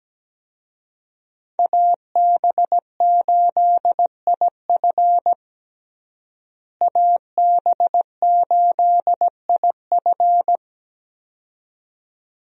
Morse code, heard as a keyed tone, about 17 wpm, 700 Hz. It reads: AB8IF AB8IF
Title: USA Call Signs - 5 Characters - Repeated 2x - 17wpm